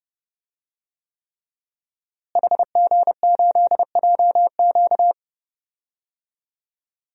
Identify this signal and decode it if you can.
Morse code, heard as a keyed tone, about 30 wpm, 700 Hz. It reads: HG8JQ
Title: DXCC-100 Call Signs - 5 Characters - Repeated 3x - 30wpm